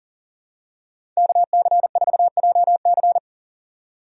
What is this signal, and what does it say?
Morse code, heard as a keyed tone, about 39 wpm, 700 Hz. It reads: KC4JC
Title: USA Call Signs - 5 Characters - 40wpm